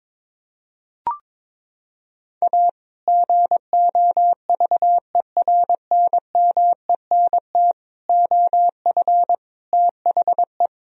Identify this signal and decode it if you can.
Morse code, heard as a keyed tone, about 22 wpm, 700 Hz. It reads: A GOVERNMENT OF THE
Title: Sentences from Top 1000 Words - Review Part 3 - 22wpm